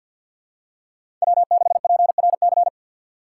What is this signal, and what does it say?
Morse code, heard as a keyed tone, about 53 wpm, 700 Hz. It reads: W6PRC
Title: USA Call Signs - 5 Characters - 3x Character Spacing - 50wpm